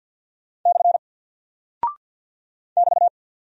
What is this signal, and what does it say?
Morse code, heard as a keyed tone, about 48 wpm, 700 Hz. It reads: / <BT>
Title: Single Character (Letter:Number:Punctuation:Prosign) 50wpm